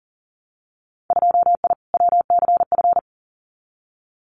Morse code, heard as a keyed tone, about 41 wpm, 700 Hz. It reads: 2I WCF
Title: Winter Field Day Exchanges 40wpm